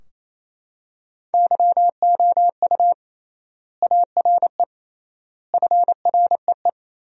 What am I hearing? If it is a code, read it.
Morse code, heard as a keyed tone, about 28 wpm, 700 Hz. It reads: YOU ARE FREE
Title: 3x Word-Spacing - Sentences from Top 500 Words - 28wpm